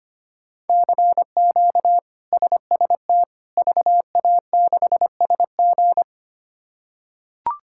Morse code, heard as a keyed tone, about 25 wpm, 700 Hz. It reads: CQ SST VA6SG
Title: K1USN SST QSO Practice - 25wpm